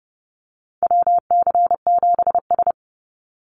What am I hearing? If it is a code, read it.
Morse code, heard as a keyed tone, about 30 wpm, 700 Hz. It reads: WC7S
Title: Call Signs - 30wpm